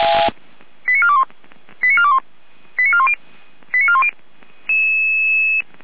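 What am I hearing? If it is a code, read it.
Five-tone selective call (ZVEI), heard as quick runs of five tones, tones of about 80 ms. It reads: 89421 89421 89410 89410
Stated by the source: Five-tone ZVEI selective call received on an SDR.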